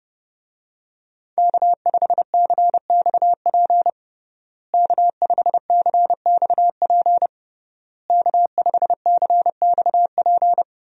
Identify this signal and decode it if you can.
Morse code, heard as a keyed tone, about 30 wpm, 700 Hz. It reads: K5CXP K5CXP K5CXP
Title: USA Call Signs - 5 Characters - Repeated 3x - 30wpm